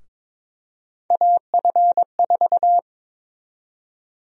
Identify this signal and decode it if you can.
Morse code, heard as a keyed tone, about 22 wpm, 700 Hz. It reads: AF4
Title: USA Call Sign Prefixes - 22wpm